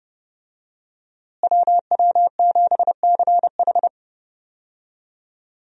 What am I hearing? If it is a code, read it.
Morse code, heard as a keyed tone, about 30 wpm, 700 Hz. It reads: WW7CH
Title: USA Call Signs - 5 Characters - Repeated 2x - 30wpm